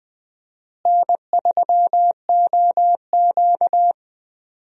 Morse code, heard as a keyed tone, about 20 wpm, 700 Hz. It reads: N3OQ
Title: USA Call Signs - 4 Characters - Repeated 2x - 20wpm